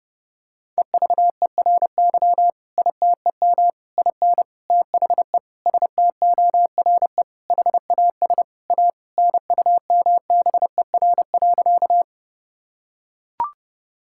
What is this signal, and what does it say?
Morse code, heard as a keyed tone, about 30 wpm, 700 Hz. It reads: EVERY ITEM IN THE STORE HAS A NUMBER.
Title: Sentences from Top 1000 Words - Review Part 6 - 30wpm